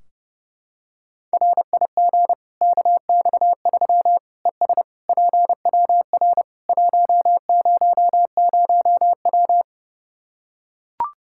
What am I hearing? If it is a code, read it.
Morse code, heard as a keyed tone, about 30 wpm, 700 Hz. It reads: RIG KX3 ES PWR 100W
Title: K1USN SST QSO Practice 30wpm